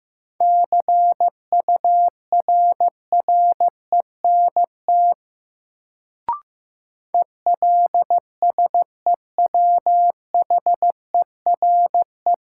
Morse code, heard as a keyed tone, about 15 wpm, 700 Hz. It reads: CURRENT ELSEWHERE
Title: Sentences from Top 1600 Words - New Word Drill Alt - 15wpm